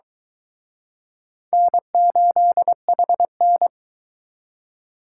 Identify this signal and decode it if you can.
Morse code, heard as a keyed tone, about 23 wpm, 700 Hz. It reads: N8HN